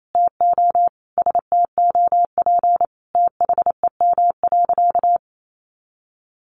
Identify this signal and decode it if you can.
Morse code, heard as a keyed tone, about 28 wpm, 700 Hz. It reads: TO STOP THEM.
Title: Sentences from Top 1100 Words 28wpm